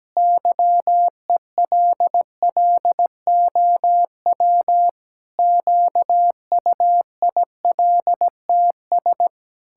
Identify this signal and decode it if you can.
Morse code, heard as a keyed tone, about 17 wpm, 700 Hz. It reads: YELLOW QUILTS
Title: Pangram Challenge v2 - 17wpm